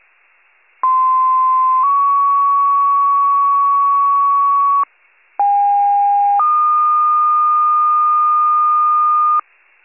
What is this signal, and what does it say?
Two-tone page, heard as a one-second tone, then a three-second tone, about 1040 then 1120 Hz, about 810 then 1220 Hz. It is heard from 0.8 to 9.4 s.